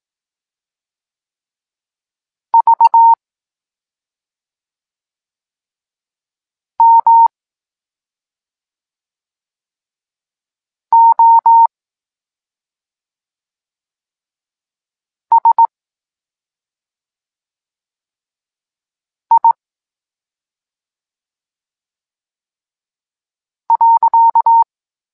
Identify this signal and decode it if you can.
Morse code, heard as a keyed tone, about 18 wpm, 920 Hz. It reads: V M O S I .
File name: Tekst-9-rownomierny